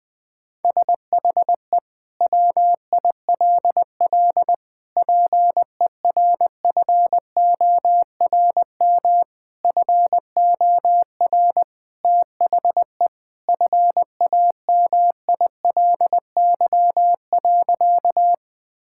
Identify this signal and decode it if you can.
Morse code, heard as a keyed tone, about 20 wpm, 700 Hz. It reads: SHE WILL PERFORM FOR THE FAMILY.
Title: Sentences from Top 1000 Words - Review Part 5 - 20wpm